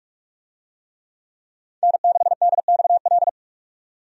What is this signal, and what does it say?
Morse code, heard as a keyed tone, about 42 wpm, 700 Hz. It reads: N6DXL
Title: USA Call Signs - 5 Characters - 45wpm